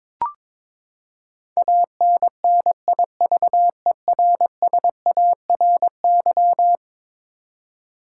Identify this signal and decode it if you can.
Morse code, heard as a keyed tone, about 22 wpm, 700 Hz. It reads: ANNIVERSARY
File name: practice / 22wpm / Eleven Letter Words - 22wpm